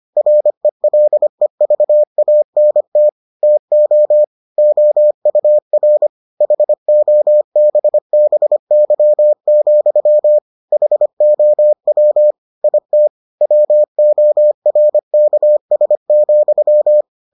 Morse code, heard as a keyed tone, about 25 wpm, 600 Hz. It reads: RELEVANT TO OUR HOBBY, HOW IT WORKS,